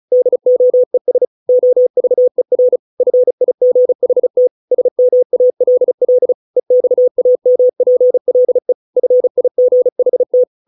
Morse code, heard as a keyed tone, about 34 wpm, 500 Hz. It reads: DOES OVER FIGHT SMALL EXAMPLE FIGHT